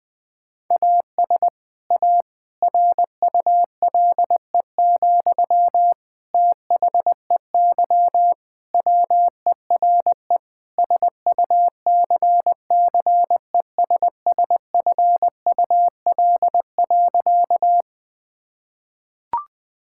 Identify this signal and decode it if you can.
Morse code, heard as a keyed tone, about 21 wpm, 700 Hz. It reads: AS A RULE, THEY WERE SUCCESSFUL.
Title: Sentences from Top 1000 Words - Review Part 6 - 20wpm